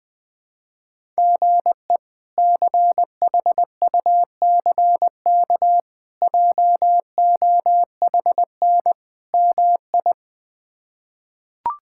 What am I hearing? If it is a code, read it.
Morse code, heard as a keyed tone, about 20 wpm, 700 Hz. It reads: GE CHUCK JOHN MI
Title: K1USN SST QSO Practice - 20wpm